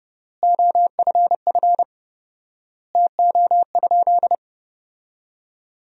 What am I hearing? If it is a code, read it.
Morse code, heard as a keyed tone, about 30 wpm, 700 Hz. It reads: OFF TO?
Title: 4x Word-Spacing - Sentences from Top 200 Words - 30wpm